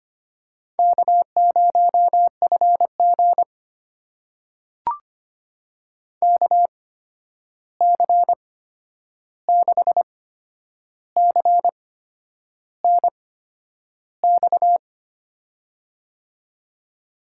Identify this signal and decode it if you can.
Morse code, heard as a keyed tone, about 25 wpm, spaced as for 9 wpm, 700 Hz. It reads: K0FG KC6CNX
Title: USA Call Signs - 8x Character Spacing - 25wpm